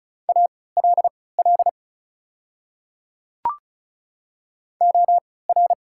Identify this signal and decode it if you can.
Morse code, heard as a keyed tone, about 36 wpm, spaced as for 20 wpm, 690 Hz. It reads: ALL OR
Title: 3x Character Spacing - Top 100 Words - 35wpm